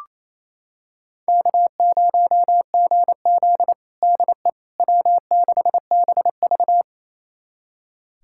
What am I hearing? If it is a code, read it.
Morse code, heard as a keyed tone, about 28 wpm, 700 Hz. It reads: K0GZ DE W6BV
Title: USA Callsign DE Callsign - 4 Characters - 28wpm